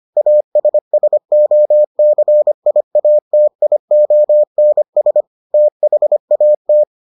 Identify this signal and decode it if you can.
Morse code, heard as a keyed tone, about 25 wpm, 600 Hz. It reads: ASSOCIATIONS THAT